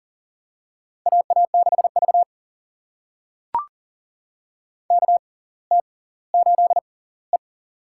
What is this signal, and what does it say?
Morse code, heard as a keyed tone, about 40 wpm, spaced as for 21 wpm, 700 Hz. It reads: AA6V KT8E
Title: USA Call Signs - 4 Characters - 6x Character Spacing - 40wpm